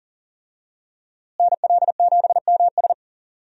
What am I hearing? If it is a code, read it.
Morse code, heard as a keyed tone, about 40 wpm, 700 Hz. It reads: NL7MS
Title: USA Call Signs - 5 Characters - Repeated 3x - 40wpm